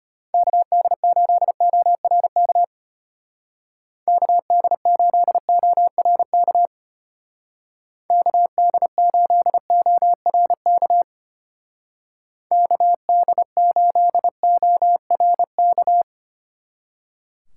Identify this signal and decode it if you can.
Morse code, heard as a keyed tone, about 31 wpm, 700 Hz. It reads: KD8ORK KD8ORK KD8ORK KD8ORK